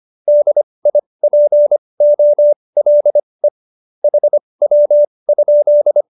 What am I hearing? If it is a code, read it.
Morse code, heard as a keyed tone, about 25 wpm, spaced as for 20 wpm, 590 Hz. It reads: DIPOLE HW?
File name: qso216b 20